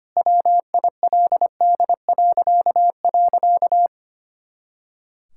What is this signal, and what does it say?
Morse code, heard as a keyed tone, about 25 wpm, 700 Hz. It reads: WILD..
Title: Pangram Challenge v2 - 25wpm